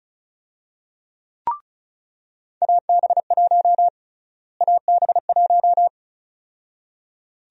Morse code, heard as a keyed tone, about 34 wpm, 700 Hz. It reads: AB1 AB1
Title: USA Call Sign Prefixes - Repeated 2x - 35wpm